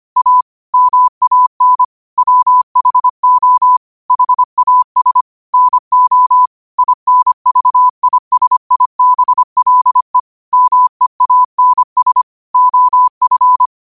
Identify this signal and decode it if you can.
Morse code, heard as a keyed tone, about 25 wpm, 1000 Hz. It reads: A MAN WHO HAS NO INVISIBLE MEANS OF